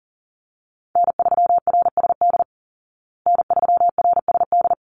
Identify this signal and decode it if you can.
Morse code, heard as a keyed tone, about 40 wpm, 700 Hz. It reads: N3RSD N3RSD
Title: Call Signs Repeated 3x - 40wpm